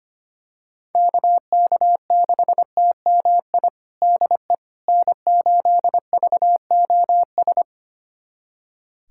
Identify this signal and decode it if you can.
Morse code, heard as a keyed tone, about 25 wpm, 700 Hz. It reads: KK6TMI DE N8VOS
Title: USA Callsign DE Callsign 25wpm